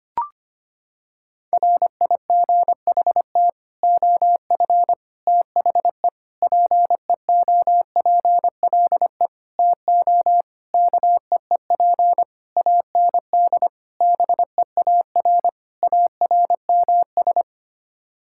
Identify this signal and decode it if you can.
Morse code, heard as a keyed tone, about 25 wpm, 700 Hz. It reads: RIGHT OF THE PEOPLE TO KEEP AND BEAR ARMS
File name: Sets of 9 Words - 25wpm